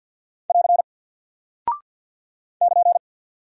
Morse code, heard as a keyed tone, about 48 wpm, 700 Hz. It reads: <AR> <KN>
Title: Single Character (Letter:Number:Punctuation:Prosign) Alt 50wpm